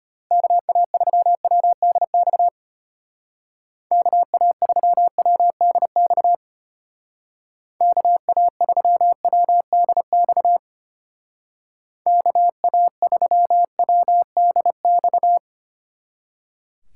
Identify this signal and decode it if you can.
Morse code, heard as a keyed tone, about 32 wpm, 700 Hz. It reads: KA3WDX KA3WDX KA3WDX KA3WDX